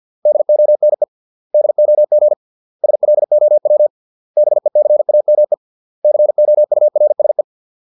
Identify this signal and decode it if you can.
Morse code, heard as a keyed tone, about 50 wpm, 600 Hz. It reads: DONE DOG SLOW BECAME COURSE